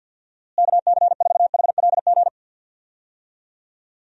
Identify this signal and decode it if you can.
Morse code, heard as a keyed tone, about 48 wpm, 700 Hz. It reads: KC4HLG
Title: USA Call Signs - 6 Characters - Repeated 3x - 50wpm